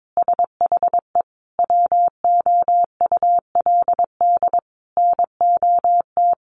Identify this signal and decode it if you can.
Morse code, heard as a keyed tone, about 22 wpm, 700 Hz. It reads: SHE WOULD NOT
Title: Sentences from Top 800 Words 22wpm